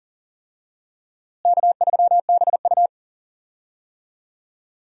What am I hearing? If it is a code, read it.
Morse code, heard as a keyed tone, about 40 wpm, 700 Hz. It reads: K3BU
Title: USA Call Signs - 4 Characters - Repeated 3x - 40wpm